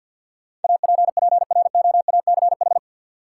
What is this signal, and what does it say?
Morse code, heard as a keyed tone, about 52 wpm, 700 Hz. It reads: APPROACH
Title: Sentences from Top 1000 Words - Review Part 6 - New Word Drill - 50wpm